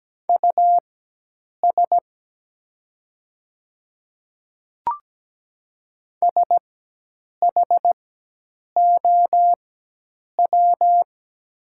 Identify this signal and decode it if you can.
Morse code, heard as a keyed tone, about 17 wpm, spaced as for 8 wpm, 700 Hz. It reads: US SHOW
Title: 4x Character Spacing - Top 200 Words - 17wpm